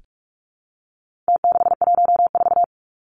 Morse code, heard as a keyed tone, about 44 wpm, 700 Hz. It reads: T614
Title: Numbers - Contest Style - 4-Digit 45wpm